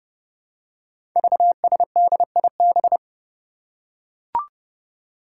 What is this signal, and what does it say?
Morse code, heard as a keyed tone, about 30 wpm, 700 Hz. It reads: VSDIB
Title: Five Challenging Letter-Number Combinations 30wpm